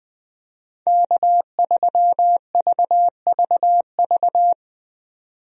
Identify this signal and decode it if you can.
Morse code, heard as a keyed tone, about 20 wpm, 700 Hz. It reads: K3VVV
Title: USA Call Signs - 5 Characters - 20wpm